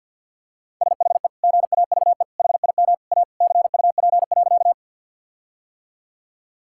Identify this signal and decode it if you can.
Morse code, heard as a keyed tone, about 51 wpm, 700 Hz. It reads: SHE GAVE HIM A CUP.
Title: Sentences from Top 1600 Words 50wpm